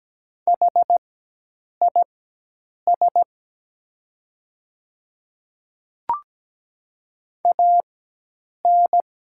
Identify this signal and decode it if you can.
Morse code, heard as a keyed tone, about 17 wpm, spaced as for 8 wpm, 700 Hz. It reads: HIS AN
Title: 4x Character Spacing - Top 100 Words - 17wpm